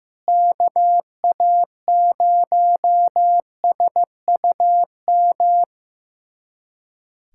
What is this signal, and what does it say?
Morse code, heard as a keyed tone, about 15 wpm, 700 Hz. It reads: KA0SUM
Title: USA Callsign DE Callsign - 6 Characters - 15wpm